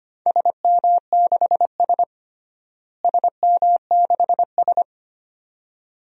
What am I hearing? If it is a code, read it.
Morse code, heard as a keyed tone, about 25 wpm, 700 Hz. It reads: SM6S SM6S
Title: DXCC-100 Call Signs - 4 Characters - Repeated 2x - 25wpm